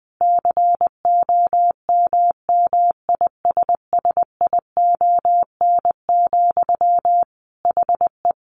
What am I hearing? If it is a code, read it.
Morse code, heard as a keyed tone, about 20 wpm, 700 Hz. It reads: COMMISSION, HE
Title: Sentences from Top 1400 Words 20wpm